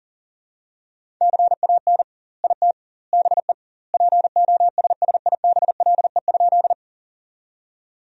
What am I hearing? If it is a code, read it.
Morse code, heard as a keyed tone, about 40 wpm, 700 Hz. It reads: CAN IT BE POSSIBLE?
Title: 2x Word-Spacing - Sentences from Top 500 Words - 40wpm